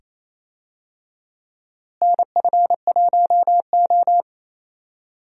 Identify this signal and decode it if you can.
Morse code, heard as a keyed tone, about 27 wpm, 700 Hz. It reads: NF1O
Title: USA Call Signs - 4 Characters - Repeated 2x - 28wpm